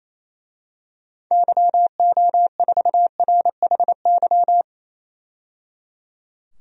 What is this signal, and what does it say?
Morse code, heard as a keyed tone, about 28 wpm, 700 Hz. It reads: YO4RHY
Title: DXCC-100 Call Signs - 6 Characters - 28wpm